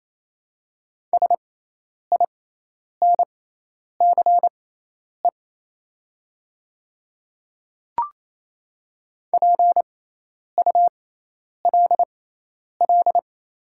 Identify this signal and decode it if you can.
Morse code, heard as a keyed tone, about 28 wpm, spaced as for 10 wpm, 700 Hz. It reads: SINCE PULL
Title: 6x Character Spacing - Top 500 Words - 28wpm